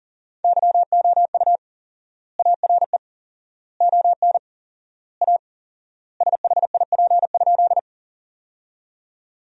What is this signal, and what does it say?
Morse code, heard as a keyed tone, about 40 wpm, 700 Hz. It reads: YOU ARE ON A SHIP?
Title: 4x Word-Spacing - Sentences from Top 400 Words - 40wpm